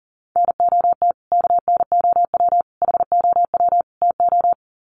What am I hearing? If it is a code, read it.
Morse code, heard as a keyed tone, about 40 wpm, 700 Hz. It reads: NOT KNOW HOW TO
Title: Sets of 4 Words 40wpm